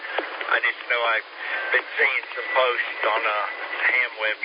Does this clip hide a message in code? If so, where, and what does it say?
Yes: Emergency Alert System (SAME) tones from 0.4 to 4.1 s, heard as harsh buzzing bursts, with speech.